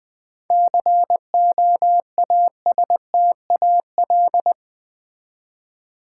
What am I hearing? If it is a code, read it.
Morse code, heard as a keyed tone, about 20 wpm, 690 Hz. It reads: COASTAL
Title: 7 Letter Words - First 3 Letters Given - 20wpm